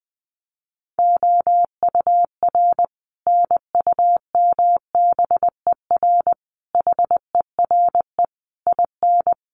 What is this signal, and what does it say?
Morse code, heard as a keyed tone, about 20 wpm, 700 Hz. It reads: OUR NUMBER HERE IN
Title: Sets of 5 Words Encore2 20wpm